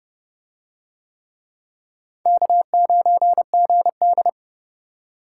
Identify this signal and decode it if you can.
Morse code, heard as a keyed tone, about 30 wpm, 700 Hz. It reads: K9GD